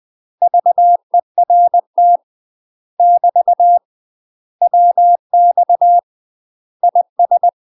Morse code, heard as a keyed tone, about 20 wpm, 700 Hz. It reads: VERT <BT> WX IS
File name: UK MASTER QSOs TRAINING / 20WPM / UK MASTER QSOs 0041